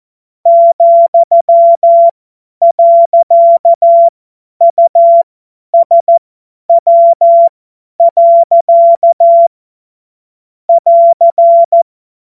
Morse code, heard as a keyed tone, about 14 wpm, spaced as for 10 wpm, 680 Hz. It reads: ,.USW. <AR>
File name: Witz-12-August-2025-50-F70